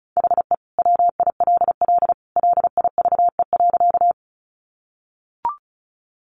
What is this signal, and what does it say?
Morse code, heard as a keyed tone, about 34 wpm, 700 Hz. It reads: HE WILL LIVE.
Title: Sentences from Top 200 Words 35wpm